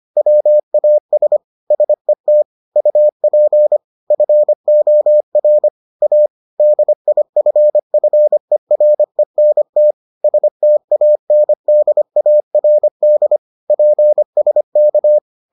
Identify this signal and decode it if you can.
Morse code, heard as a keyed tone, about 25 wpm, 600 Hz. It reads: WAS SET UP FOR A DIFFERENT STANDARD PSK